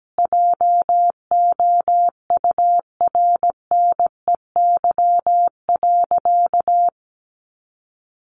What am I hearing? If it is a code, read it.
Morse code, heard as a keyed tone, about 17 wpm, 700 Hz. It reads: JOURNEY.
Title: A Sound Of Thunder 17wpm